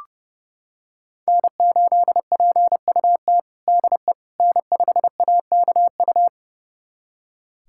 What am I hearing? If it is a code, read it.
Morse code, heard as a keyed tone, about 30 wpm, 700 Hz. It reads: N8PUT DE N5AKU
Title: USA Callsign DE Callsign - 5 Characters - 30wpm